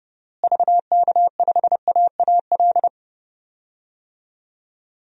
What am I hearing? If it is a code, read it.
Morse code, heard as a keyed tone, about 30 wpm, 700 Hz. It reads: VK5AAL
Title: DXCC-100 Call Signs - 6 Characters - Repeated 2x - 30wpm